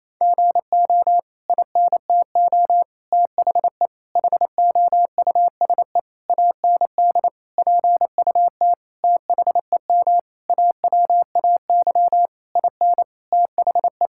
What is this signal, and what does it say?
Morse code, heard as a keyed tone, about 28 wpm, 700 Hz. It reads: GO INTO THE HOUSE AND PUT THEM AWAY IN THE
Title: Sentences from Top 1000 Words - Review Part 5 - 28wpm